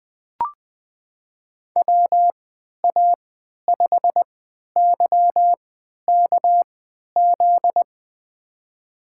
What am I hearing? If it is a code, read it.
Morse code, heard as a keyed tone, about 20 wpm, spaced as for 11 wpm, 700 Hz. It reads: WA5YKZ
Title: USA Call Signs - 6 Characters - 3x Character Spacing - 20wpm